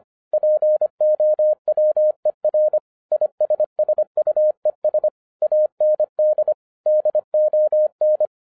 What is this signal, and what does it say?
Morse code, heard as a keyed tone, about 25 wpm, 610 Hz. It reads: POWER ISSUES AND DON
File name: 2011-07-mp3 / n39 THE 3.11 2 25wpm